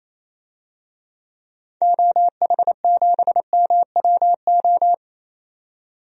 Morse code, heard as a keyed tone, about 28 wpm, 700 Hz. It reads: OH7MWO
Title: DXCC-100 Call Signs - 6 Characters - 28wpm